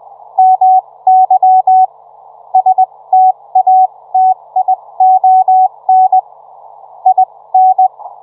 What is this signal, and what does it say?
Morse code, heard as a keyed tone, about 20 wpm, spaced as for 17 wpm, 750 Hz. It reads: MY STATION IN